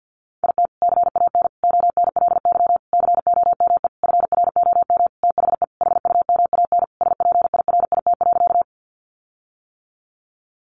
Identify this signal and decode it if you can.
Morse code, heard as a keyed tone, about 48 wpm, 700 Hz. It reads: IT CAN ONLY COME FROM THE HUMAN SPIRIT.
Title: Sentences from Top 1400 Words 50wpm